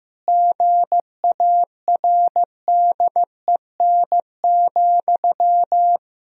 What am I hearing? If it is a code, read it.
Morse code, heard as a keyed tone, about 16 wpm, 700 Hz. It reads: GARDEN,
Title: Sentences from Top 1000 Words - Review Part 4 - 15wpm